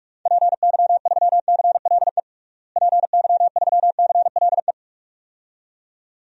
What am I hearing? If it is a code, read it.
Morse code, heard as a keyed tone, about 42 wpm, 700 Hz. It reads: PY3CLE PY3CLE
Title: DXCC-100 Call Signs - 6 Characters - Repeated 2x - 45wpm